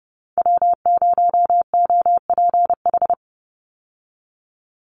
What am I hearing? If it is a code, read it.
Morse code, heard as a keyed tone, about 30 wpm, 700 Hz. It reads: W0OPH
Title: USA Call Signs 30wpm